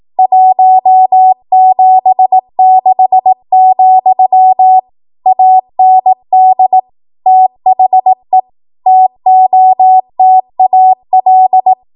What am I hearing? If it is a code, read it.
Morse code, heard as a keyed tone, about 18 wpm, 750 Hz. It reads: 176, AND THE TOTAL
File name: Archive / 18 WPM / 18WPM